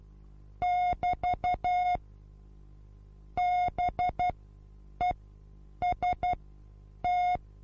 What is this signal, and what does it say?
Morse code, heard as a keyed tone, about 12 wpm, spaced as for 8 wpm, 720 Hz. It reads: <BT> BEST
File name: Onlinelehrgang / morsen / DJ4UF Morsen23